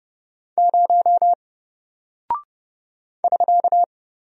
Morse code, heard as a keyed tone, about 30 wpm, 700 Hz. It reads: 0 <SK>
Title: Single Character (Letter:Number:Punctuation:Prosign) Alt 30wpm